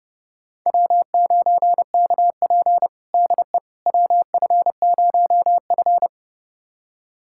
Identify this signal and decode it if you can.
Morse code, heard as a keyed tone, about 30 wpm, 700 Hz. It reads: W9KP DE WF0F
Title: USA Callsign DE Callsign - 4 Characters - 30wpm